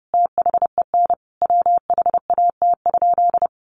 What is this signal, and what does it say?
Morse code, heard as a keyed tone, about 30 wpm, 700 Hz. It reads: THEN WHAT?
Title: Sentences from Top 1500 Words 30wpm